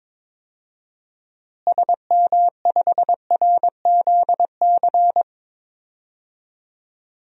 Morse code, heard as a keyed tone, about 21 wpm, 700 Hz. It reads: SM5RZC
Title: DXCC-100 Call Signs - 6 Characters - Repeated 2x - 22wpm